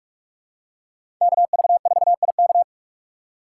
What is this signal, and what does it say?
Morse code, heard as a keyed tone, about 46 wpm, 700 Hz. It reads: KV4IK
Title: USA Call Signs - 6x Character Spacing - 45wpm